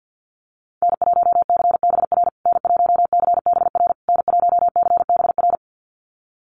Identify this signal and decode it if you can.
Morse code, heard as a keyed tone, about 50 wpm, 700 Hz. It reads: N1CBR N1CBR N1CBR
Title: USA Call Signs Repeated 3x 50wpm